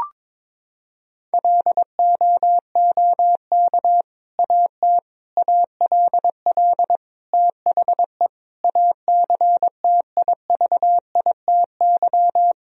Morse code, heard as a keyed tone, about 21 wpm, 700 Hz. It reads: LOOK AT ALL THE ACTIVITY
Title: Sentences from Top 1000 Words - Review Part 4 - 22wpm